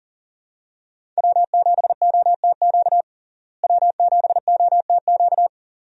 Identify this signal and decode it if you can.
Morse code, heard as a keyed tone, about 38 wpm, 700 Hz. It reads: W7OTQ W7OTQ
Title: USA Call Signs - 5 Characters - Repeated 2x - 40wpm